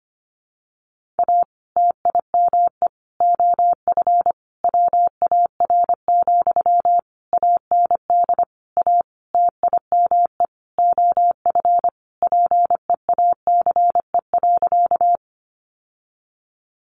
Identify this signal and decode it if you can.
Morse code, heard as a keyed tone, about 25 wpm, 700 Hz. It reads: A TIME OF WAR, AND A TIME OF PEACE.
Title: Sentences from Top 1500 Words 25wpm